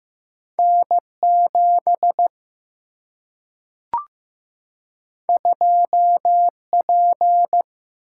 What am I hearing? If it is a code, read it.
Morse code, heard as a keyed tone, about 15 wpm, 700 Hz. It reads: N7 2P
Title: Single-Letter Single-Number 15wpm